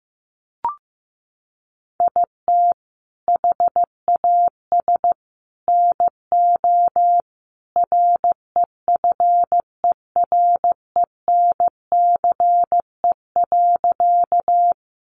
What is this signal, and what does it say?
Morse code, heard as a keyed tone, about 15 wpm, 700 Hz. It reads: IT HAS NO REFERENCE.
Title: Sentences from Top 1200 Words 15wpm